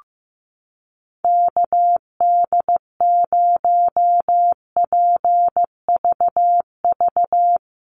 Morse code, heard as a keyed tone, about 14 wpm, 700 Hz. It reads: KD0PVV
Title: USA Call Signs Repeated 3x 15wpm